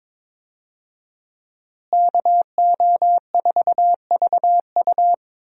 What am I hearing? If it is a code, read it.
Morse code, heard as a keyed tone, about 22 wpm, 700 Hz. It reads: KO4VU
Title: USA Call Signs - 5 Characters - Repeated 2x - 22wpm